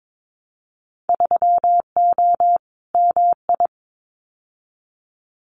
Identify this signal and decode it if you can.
Morse code, heard as a keyed tone, about 22 wpm, 700 Hz. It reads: 3O MI
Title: Winter Field Day Exchanges 22wpm